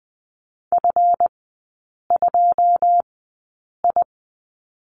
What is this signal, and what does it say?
Morse code, heard as a keyed tone, about 20 wpm, spaced as for 8 wpm, 700 Hz. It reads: F2I
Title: 138 Lesson 23e - Letter f - Three Characters 20wpm